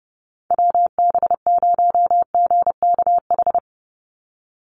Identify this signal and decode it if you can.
Morse code, heard as a keyed tone, about 30 wpm, 700 Hz. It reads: WB0GKH
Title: Call Signs Repeated 3x - 30wpm